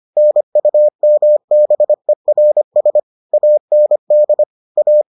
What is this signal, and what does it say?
Morse code, heard as a keyed tone, about 25 wpm, 600 Hz. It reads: NUMBERS AND A